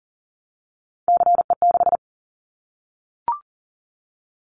Morse code, heard as a keyed tone, about 40 wpm, 700 Hz. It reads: CE6
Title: Callsign Prefixes for Top 100 DXCC - 40wpm